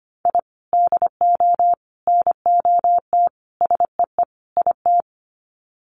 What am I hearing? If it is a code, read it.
Morse code, heard as a keyed tone, about 25 wpm, 700 Hz. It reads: I DO NOT SEE IT
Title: Sets of 5 Words Encore2 25wpm